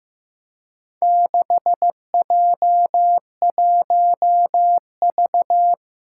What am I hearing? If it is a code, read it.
Morse code, heard as a keyed tone, about 15 wpm, 700 Hz. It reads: 6J1V
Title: Four Challenging Letter-Number Combinations 15wpm